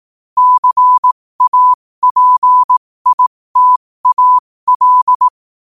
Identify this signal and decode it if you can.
Morse code, heard as a keyed tone, about 18 wpm, spaced as for 15 wpm, 1000 Hz. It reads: CAPITAL
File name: qotdfiles / 15wpm